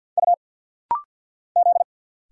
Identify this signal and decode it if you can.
Morse code, heard as a keyed tone, about 50 wpm, 700 Hz. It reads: U Z
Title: Single Letters - Rapid-Fire 50wpm